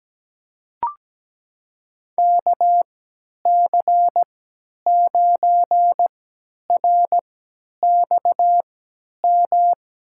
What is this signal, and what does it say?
Morse code, heard as a keyed tone, about 17 wpm, spaced as for 10 wpm, 700 Hz. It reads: KC9RXM
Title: USA Call Signs - 6 Characters - 3x Character Spacing - 17wpm